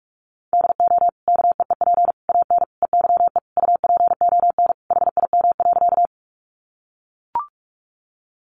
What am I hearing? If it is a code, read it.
Morse code, heard as a keyed tone, about 44 wpm, 700 Hz. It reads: DO KEEP AN EYE UPON HIM.
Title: Sentences from Top 1100 Words 45wpm